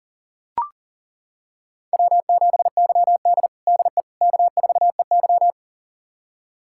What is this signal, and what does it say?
Morse code, heard as a keyed tone, about 40 wpm, 700 Hz. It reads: W7YD DE K4EY
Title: USA Callsign DE Callsign - 4 Characters - 40wpm